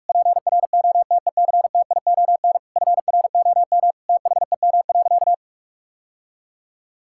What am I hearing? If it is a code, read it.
Morse code, heard as a keyed tone, about 46 wpm, 700 Hz. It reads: PROTECTION FROM THEM.
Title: Sentences from Top 1600 Words 45wpm